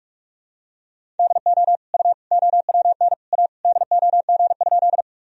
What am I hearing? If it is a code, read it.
Morse code, heard as a keyed tone, about 46 wpm, 700 Hz. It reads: DO U OWN A DOG?
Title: Sentences from Top 400 Words and Abbreviations 45wpm